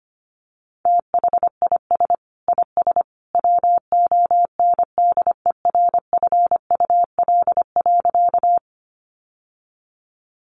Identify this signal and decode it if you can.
Morse code, heard as a keyed tone, about 25 wpm, 700 Hz. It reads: THIS IS WONDERFUL.
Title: Sentences from Top 1100 Words 25wpm